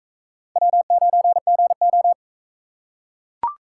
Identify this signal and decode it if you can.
Morse code, heard as a keyed tone, about 43 wpm, 700 Hz. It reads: W9GO